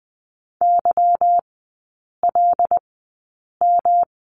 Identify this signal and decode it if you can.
Morse code, heard as a keyed tone, about 20 wpm, spaced as for 8 wpm, 700 Hz. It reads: YLM
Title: 191 Lesson 31e - Letter v - Three Characters 20wpm